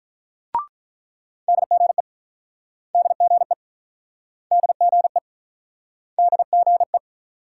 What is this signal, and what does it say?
Morse code, heard as a keyed tone, about 48 wpm, 700 Hz. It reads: DGE DGE DGE DGE